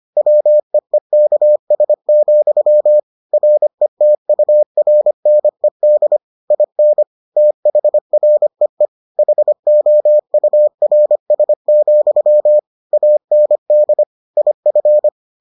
Morse code, heard as a keyed tone, about 25 wpm, 600 Hz. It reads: WEEKS, RETURNED IN THREE HOURS, AND IF